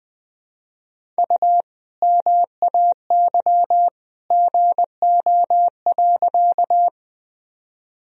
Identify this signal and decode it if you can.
Morse code, heard as a keyed tone, about 20 wpm, 700 Hz. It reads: U MAY GO.
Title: Sentences from Top 100 Words and Abbreviations 20wpm